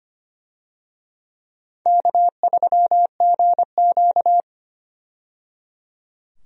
Morse code, heard as a keyed tone, about 25 wpm, 700 Hz. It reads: K3GQ
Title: USA Call Signs - 4 Characters - 25wpm